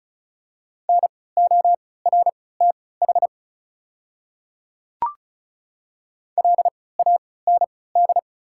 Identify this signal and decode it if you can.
Morse code, heard as a keyed tone, about 36 wpm, spaced as for 20 wpm, 700 Hz. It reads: NORTH LAND
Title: 3x Character Spacing - Top 500 Words - 35wpm